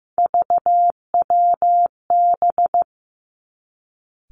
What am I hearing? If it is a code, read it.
Morse code, heard as a keyed tone, about 15 wpm, 700 Hz. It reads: VWB
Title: Three Challenging Letter-Number Combinations 15wpm